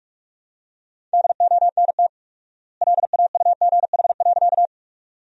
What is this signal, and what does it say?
Morse code, heard as a keyed tone, about 44 wpm, 690 Hz. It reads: DONT LAUGH.
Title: 4x Word-Spacing - Sentences from Top 500 Words - 45wpm